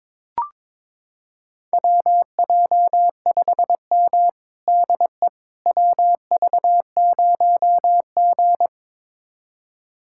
Morse code, heard as a keyed tone, about 22 wpm, 700 Hz. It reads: WJ5M DE WV0G
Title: USA Callsign DE Callsign - 4 Characters - 22wpm